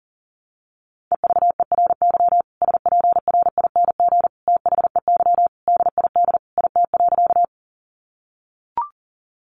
Morse code, heard as a keyed tone, about 40 wpm, 700 Hz. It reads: EVERY SPRING THEY DID IT.
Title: Sentences from Top 1200 Words 40wpm